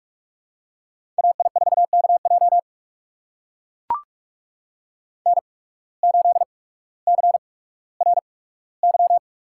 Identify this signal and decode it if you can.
Morse code, heard as a keyed tone, about 44 wpm, spaced as for 19 wpm, 700 Hz. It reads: AI4KJ N8CRY
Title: USA Call Signs - 5 Characters - 8x Character Spacing - 45wpm